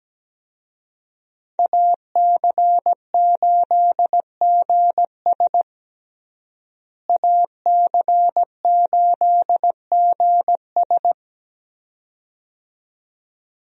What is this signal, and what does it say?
Morse code, heard as a keyed tone, about 17 wpm, 700 Hz. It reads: AC8GS AC8GS
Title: USA Call Signs - 5 Characters - Repeated 2x - 17wpm